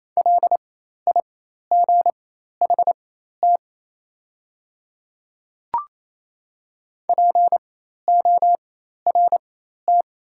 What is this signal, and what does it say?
Morse code, heard as a keyed tone, about 28 wpm, spaced as for 13 wpm, 690 Hz. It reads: LIGHT PORT
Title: 4x Character Spacing - Top 200 Words - 28wpm